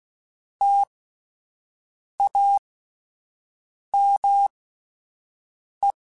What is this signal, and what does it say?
Morse code, heard as a keyed tone, about 16 wpm, spaced as for 5 wpm, 800 Hz. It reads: TAME